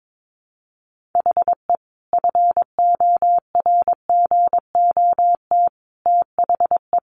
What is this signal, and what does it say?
Morse code, heard as a keyed tone, about 22 wpm, 700 Hz. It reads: HE FORGOT THE
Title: Sentences from Top 1400 Words 22wpm